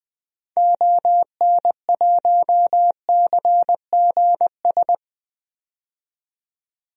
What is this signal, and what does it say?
Morse code, heard as a keyed tone, about 21 wpm, 700 Hz. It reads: ON1CGS
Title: DXCC-100 Call Signs - 6 Characters - 20wpm